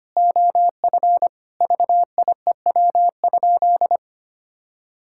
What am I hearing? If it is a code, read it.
Morse code, heard as a keyed tone, about 25 wpm, 700 Hz. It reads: OF VIEW?
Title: Sentences from Top 1000 Words - Review Part 6 - 25wpm